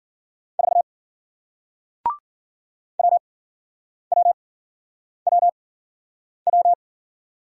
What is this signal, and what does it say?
Morse code, heard as a keyed tone, about 53 wpm, spaced as for 9 wpm, 700 Hz. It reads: 4 WWWW